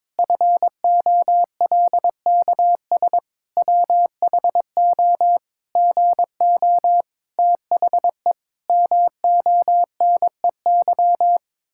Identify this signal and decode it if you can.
Morse code, heard as a keyed tone, about 22 wpm, 700 Hz. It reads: FOLKS WHO GO THE MONEY
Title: Pangram Challenge v2 - 22wpm